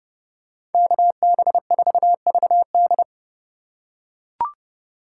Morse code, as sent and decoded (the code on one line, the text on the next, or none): -.- -... ....- ...- -..
KB4VD